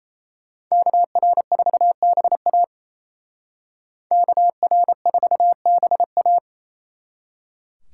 Morse code, heard as a keyed tone, about 36 wpm, 700 Hz. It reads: KR4BA KR4BA